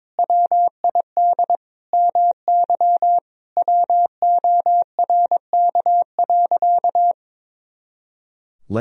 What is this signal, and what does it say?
Morse code, heard as a keyed tone, about 22 wpm, 700 Hz. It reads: WID MY WORK.
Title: Sentences from Top 300 Words and Abbreviations 22wpm